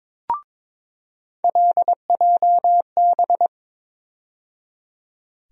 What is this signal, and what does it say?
Morse code, heard as a keyed tone, about 21 wpm, 700 Hz. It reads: LJB
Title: USA Call Sign Postfixes - 3 Letters 22wpm